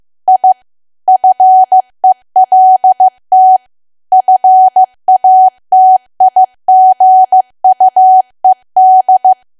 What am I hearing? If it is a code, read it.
Morse code, heard as a keyed tone, about 15 wpm, 750 Hz. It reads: I FELT FATIGUED
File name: cwa / W1AW-Level2 / 15WPM